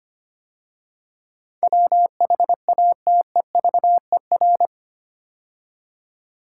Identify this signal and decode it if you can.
Morse code, heard as a keyed tone, about 25 wpm, 700 Hz. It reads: WHATEVER
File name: 8 Letter Words - First 3 Letters Given - 25wpm